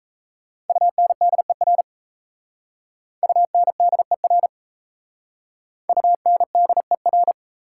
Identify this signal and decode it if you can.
Morse code, heard as a keyed tone, about 37 wpm, 690 Hz. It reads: UNDER UNDER UNDER